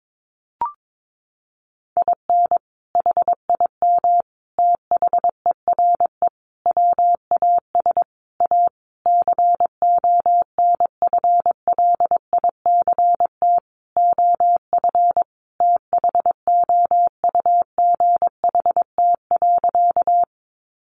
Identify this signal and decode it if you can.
Morse code, heard as a keyed tone, about 22 wpm, 700 Hz. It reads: IN HIM THERE WAS A CONFLICT OF THOUGHT.
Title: Sentences from Top 1300 Words 22wpm